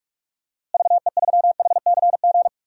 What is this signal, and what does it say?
Morse code, heard as a keyed tone, about 46 wpm, 700 Hz. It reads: VE3HCG
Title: DXCC-100 Call Signs - 6 Characters - Repeated 3x - 45wpm